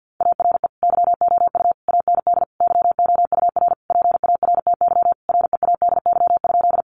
Morse code, heard as a keyed tone, about 50 wpm, 700 Hz. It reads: ARE YOU AND YOUR PARTY READY?
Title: Sentences from Top 1400 Words 50wpm